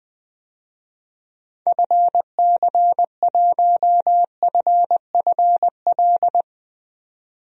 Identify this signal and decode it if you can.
Morse code, heard as a keyed tone, about 20 wpm, 700 Hz. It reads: FC1FFL
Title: DXCC-100 Call Signs - 6 Characters - 20wpm